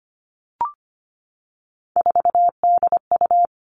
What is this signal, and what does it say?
Morse code, heard as a keyed tone, about 25 wpm, 700 Hz. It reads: 4DU
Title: Three Challenging Letter-Number Combinations 25wpm